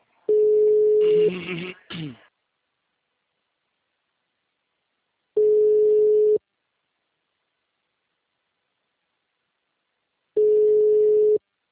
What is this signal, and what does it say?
Ringback tone (Europe, 425 Hz).
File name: out- -2034